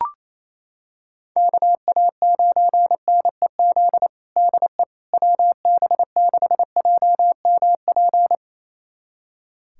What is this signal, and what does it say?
Morse code, heard as a keyed tone, about 27 wpm, 700 Hz. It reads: KA9NEZ DE WB6JMP
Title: USA Callsign DE Callsign 28wpm